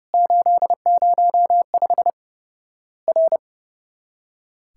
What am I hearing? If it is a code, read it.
Morse code, heard as a keyed tone, about 30 wpm, 700 Hz. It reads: 805 R
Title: Worked All Europe DX Contest - QTC Exchange - 30wpm